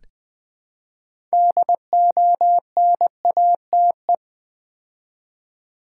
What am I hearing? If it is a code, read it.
Morse code, heard as a keyed tone, about 20 wpm, 700 Hz. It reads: DONATE
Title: Build A Word - 3 Letters At A Time - 6-Letter Words - 20wpm